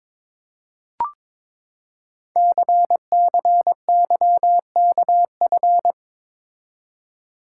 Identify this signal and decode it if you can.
Morse code, heard as a keyed tone, about 22 wpm, 700 Hz. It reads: CCYKF
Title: Code Groups - 5 Letters - 22wpm